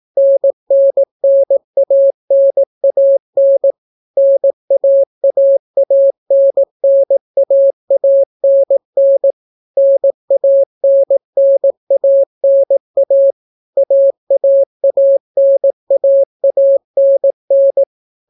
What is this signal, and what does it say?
Morse code, heard as a keyed tone, about 18 wpm, 550 Hz. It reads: NNNANAN NAAANNAANN NANNANA AAANAANN